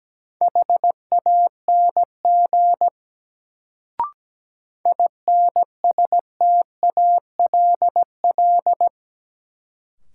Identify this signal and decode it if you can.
Morse code, heard as a keyed tone, about 17 wpm, 700 Hz. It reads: HANG INSTALL
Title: Sentences from Top 1600 Words - New Word Drill Alt - 17wpm